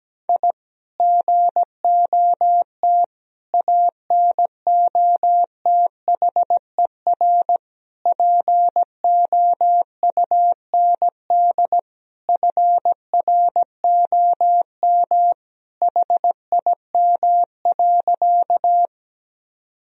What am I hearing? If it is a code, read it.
Morse code, heard as a keyed tone, about 17 wpm, 700 Hz. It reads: I GOT ANOTHER POUND FROM HIM.
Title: Sentences from Top 1000 Words - Review Part 2 - 17wpm